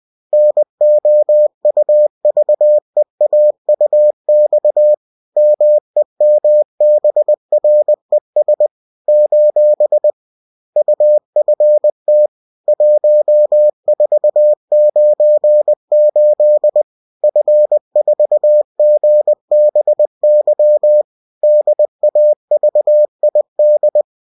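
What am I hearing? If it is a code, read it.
Morse code, heard as a keyed tone, about 20 wpm, 600 Hz. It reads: NOUVEAUX MEMBRES : UFT 1498 F4GBY DAVID
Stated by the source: Bulletin sent in CW.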